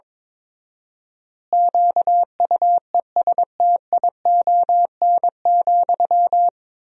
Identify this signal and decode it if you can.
Morse code, heard as a keyed tone, about 22 wpm, 700 Hz. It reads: QUESTION,
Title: 4x Word-Spacing - Sentences from Top 400 Words - 22wpm